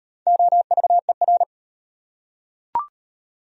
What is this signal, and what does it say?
Morse code, heard as a keyed tone, about 38 wpm, 700 Hz. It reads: OVER